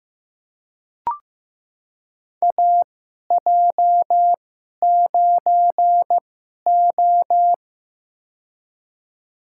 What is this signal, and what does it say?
Morse code, heard as a keyed tone, about 15 wpm, spaced as for 11 wpm, 700 Hz. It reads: AJ9O
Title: USA Call Signs - 2x Character Spacing - 15wpm